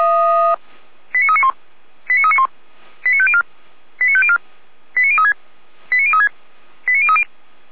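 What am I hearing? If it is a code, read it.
Five-tone selective call (ZVEI), heard as quick runs of five tones, tones of about 70 ms. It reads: 89391 89391 89504 89504 89036 89036 89030
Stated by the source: Five-tone ZVEI selective call received by SDR, fire brigade alerting, Feuerwehr Orlamünde.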